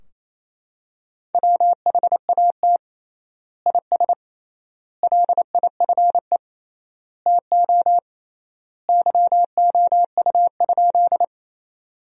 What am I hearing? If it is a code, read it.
Morse code, heard as a keyed tone, about 27 wpm, 700 Hz. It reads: WHAT IS LIFE TO YOU?
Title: 3x Word-Spacing - Sentences from Top 300 Words - 28wpm